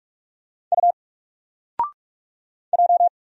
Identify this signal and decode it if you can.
Morse code, heard as a keyed tone, about 45 wpm, 700 Hz. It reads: U J
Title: Single Letter-Number 45wpm